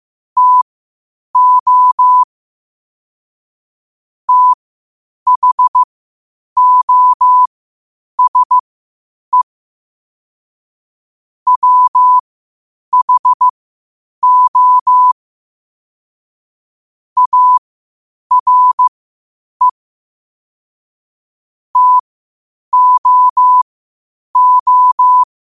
Morse code, heard as a keyed tone, about 15 wpm, spaced as for 8 wpm, 1000 Hz. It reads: TO THOSE WHO ARE TOO
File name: qotdfiles / 8wpm